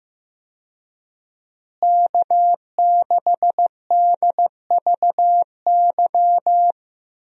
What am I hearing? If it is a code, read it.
Morse code, heard as a keyed tone, about 15 wpm, 700 Hz. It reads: K6DVY
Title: USA Call Signs - 5 Characters - Repeated 3x - 15wpm